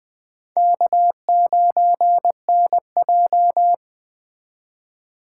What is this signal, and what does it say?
Morse code, heard as a keyed tone, about 19 wpm, 700 Hz. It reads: K9NJ